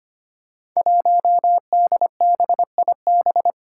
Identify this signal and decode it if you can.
Morse code, heard as a keyed tone, about 25 wpm, 700 Hz. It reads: 1DBIB
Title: Five Challenging Letter-Number Combinations 25wpm